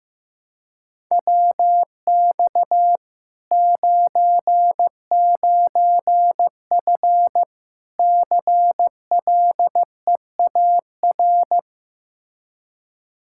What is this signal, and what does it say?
Morse code, heard as a keyed tone, about 16 wpm, 700 Hz. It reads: WX 99F CLEAR
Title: Practice QSO - Exchange 2 - Info - 15wpm